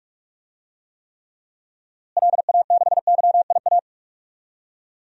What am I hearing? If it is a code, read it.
Morse code, heard as a keyed tone, about 42 wpm, 700 Hz. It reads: LA6YIA
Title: DXCC-100 Call Signs - 6 Characters - Repeated 3x - 45wpm